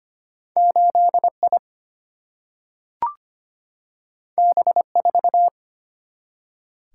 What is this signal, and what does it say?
Morse code, heard as a keyed tone, about 25 wpm, 700 Hz. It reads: 8 I B 4
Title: Single-Letter Single-Number 25wpm